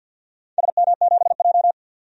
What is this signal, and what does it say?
Morse code, heard as a keyed tone, about 48 wpm, 700 Hz. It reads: SM7J
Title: DXCC-100 Call Signs - 4 Characters - 50wpm